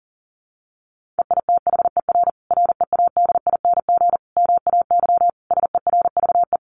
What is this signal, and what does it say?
Morse code, heard as a keyed tone, about 40 wpm, 700 Hz. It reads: EITHER READING MAY SERVE
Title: Sentences from Top 1400 Words 40wpm